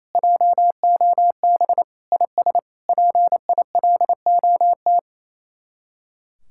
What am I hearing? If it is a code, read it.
Morse code, heard as a keyed tone, about 36 wpm, 700 Hz. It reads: JOB IS PILOT